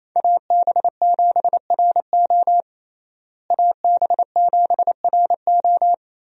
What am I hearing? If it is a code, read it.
Morse code, heard as a keyed tone, about 28 wpm, 700 Hz. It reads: AB7RO AB7RO
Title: USA Call Signs - 5 Characters - Repeated 3x - 28wpm